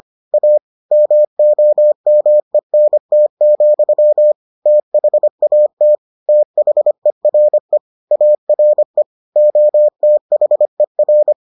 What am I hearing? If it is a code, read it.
Morse code, heard as a keyed tone, about 25 wpm, 600 Hz. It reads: A MOMENT, THAT THERE ARE OTHER